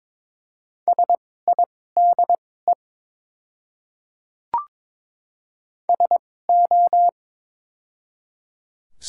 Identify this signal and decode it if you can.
Morse code, heard as a keyed tone, about 21 wpm, spaced as for 16 wpm, 700 Hz. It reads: SIDE SO
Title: 2x Character Spacing - Top 100 Words - 22wpm